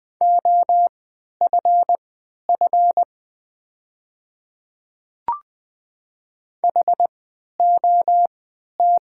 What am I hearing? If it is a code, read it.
Morse code, heard as a keyed tone, about 20 wpm, spaced as for 11 wpm, 690 Hz. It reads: OFF HOT
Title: 3x Character Spacing - Top 300 Words - 20wpm